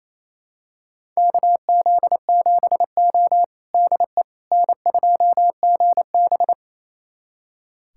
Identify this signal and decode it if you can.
Morse code, heard as a keyed tone, about 27 wpm, 690 Hz. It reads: KZ7O DE N2GB
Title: USA Callsign DE Callsign - 4 Characters - 28wpm